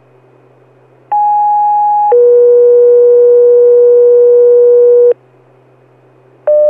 Two-tone page, heard as a one-second tone, then a three-second tone, about 830 then 480 Hz. It is heard from 1.1 to 5.1 s.